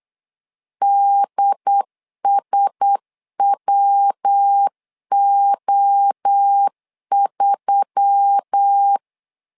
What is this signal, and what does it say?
Morse code, heard as a keyed tone, about 8 wpm, 800 Hz. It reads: DSWO3
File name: LEZIONI / 4 Q1S3 8wpm 3037